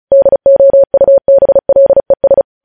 Morse code, heard as a keyed tone, about 36 wpm, 550 Hz. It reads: DOUBLES
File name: podcast / News headlines-1x-35x35